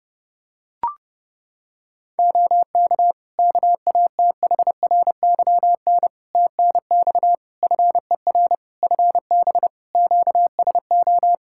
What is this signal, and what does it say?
Morse code, heard as a keyed tone, about 30 wpm, 700 Hz. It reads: OK KATHRYN TNX FER FB QSO
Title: K1USN SST QSO Practice 30wpm